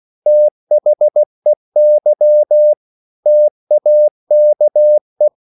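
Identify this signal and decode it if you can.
Morse code, heard as a keyed tone, about 16 wpm, 600 Hz. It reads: THEY TAKE
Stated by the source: English quotations sent in CW as practice at 16 words per minute.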